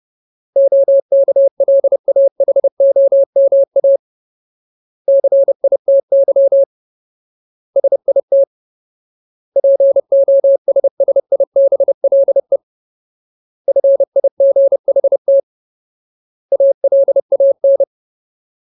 Morse code, heard as a keyed tone, about 30 wpm, 550 Hz. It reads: OKLAHOMA CITY SIT POSSIBLE FIGHT ALAN
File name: Ultimate Word List / 30WPM Ultimate